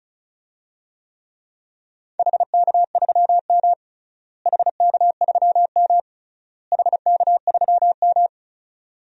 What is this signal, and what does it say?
Morse code, heard as a keyed tone, about 36 wpm, 700 Hz. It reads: HK3M HK3M HK3M
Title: DXCC-100 Call Signs - 4 Characters - Repeated 3x - 35wpm